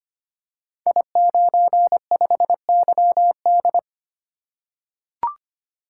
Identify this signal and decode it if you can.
Morse code, heard as a keyed tone, about 25 wpm, 700 Hz. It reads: I95YD
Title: Code Groups - 5 Letter:Numbers - 25wpm